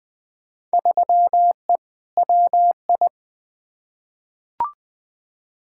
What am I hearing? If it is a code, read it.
Morse code, heard as a keyed tone, about 20 wpm, 700 Hz. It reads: 3E WI
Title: Field Day Exchanges Encore - 20wpm